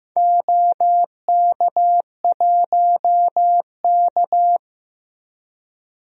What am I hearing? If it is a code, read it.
Morse code, heard as a keyed tone, about 15 wpm, 700 Hz. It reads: OK1K
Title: DXCC-100 Call Signs - 4 Characters - Repeated 3x - 15wpm